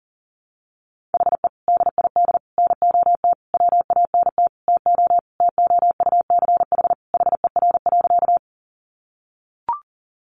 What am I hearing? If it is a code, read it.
Morse code, heard as a keyed tone, about 40 wpm, 700 Hz. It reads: HE DID NOT WANT TO TOUCH HER.
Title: Sentences from Top 1200 Words 40wpm